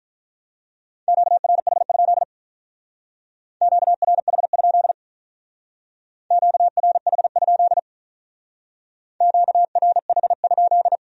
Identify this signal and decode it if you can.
Morse code, heard as a keyed tone, about 46 wpm, 700 Hz. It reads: QRH? QRH? QRH? QRH?